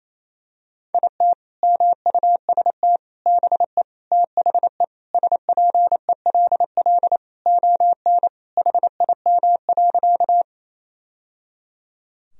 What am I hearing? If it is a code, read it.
Morse code, heard as a keyed tone, about 27 wpm, 700 Hz. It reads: IT MUST BE THE SPELL ON HIM.
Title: Sentences from Top 1000 Words - Review Part 1 - 28wpm